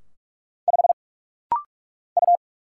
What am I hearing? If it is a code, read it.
Morse code, heard as a keyed tone, about 46 wpm, 700 Hz. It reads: 5 U
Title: Single Letter-Number - Rapid-Fire 45wpm